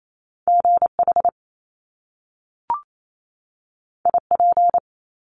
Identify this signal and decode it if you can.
Morse code, heard as a keyed tone, about 28 wpm, 700 Hz. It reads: GH IP
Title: Top Two Letter Combinations in Words 28wpm